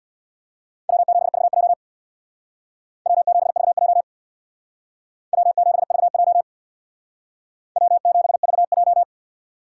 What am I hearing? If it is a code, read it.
Morse code, heard as a keyed tone, about 60 wpm, 700 Hz. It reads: W7VJ W7VJ W7VJ W7VJ
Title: Speed Racing - USA Call Signs 50wpm